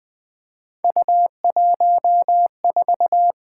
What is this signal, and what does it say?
Morse code, heard as a keyed tone, about 20 wpm, 700 Hz. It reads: U14
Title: Three Letter-Number Combinations - 20wpm